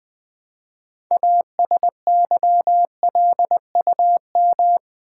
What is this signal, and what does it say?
Morse code, heard as a keyed tone, about 20 wpm, 700 Hz. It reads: ASYLUM
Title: Build A Word - 2 Letters At A Time - 6-Letter Words - 20wpm